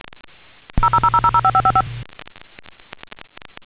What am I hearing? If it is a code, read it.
DTMF, heard as telephone touch tones, tones of about 70 ms. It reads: 000005555